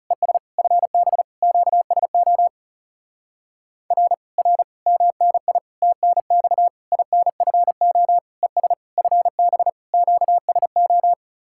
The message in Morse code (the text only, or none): ES FB QSO R R MNI TNX INFO ES FB QSO